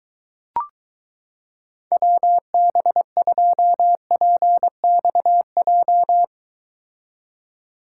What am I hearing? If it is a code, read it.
Morse code, heard as a keyed tone, about 23 wpm, 700 Hz. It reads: WB2PXJ